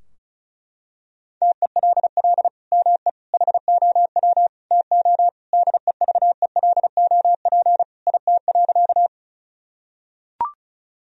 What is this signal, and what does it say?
Morse code, heard as a keyed tone, about 34 wpm, 700 Hz. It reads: TELL ME HOW TO DEVELOP IT.
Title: Sentences from Top 1000 Words - Review Part 2 - 35wpm